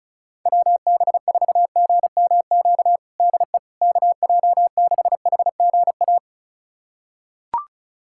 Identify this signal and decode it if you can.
Morse code, heard as a keyed tone, about 36 wpm, 700 Hz. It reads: WB4GMQ DE KJ6HGA
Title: USA Callsign DE Callsign - 6 Characters - 35wpm